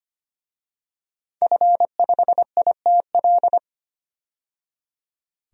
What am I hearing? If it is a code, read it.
Morse code, heard as a keyed tone, about 25 wpm, 690 Hz. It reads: F5ITL
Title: DXCC-100 Call Signs - 5 Characters - 25wpm